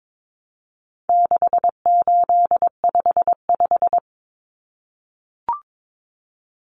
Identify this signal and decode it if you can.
Morse code, heard as a keyed tone, about 22 wpm, 700 Hz. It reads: 6855
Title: Common Cut Numbers - 4-Digit 22wpm